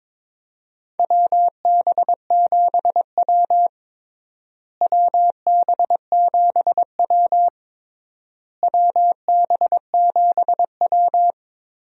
Morse code, heard as a keyed tone, about 22 wpm, 700 Hz. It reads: WB7W WB7W WB7W
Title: USA Call Signs - 4 Characters - Repeated 3x - 22wpm